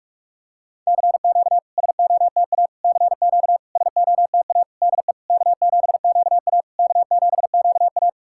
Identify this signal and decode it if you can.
Morse code, heard as a keyed tone, about 42 wpm, 690 Hz. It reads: CQ SOTA CQ SOTA DE K7QA K7QA
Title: SOTA QSO Practice 45wpm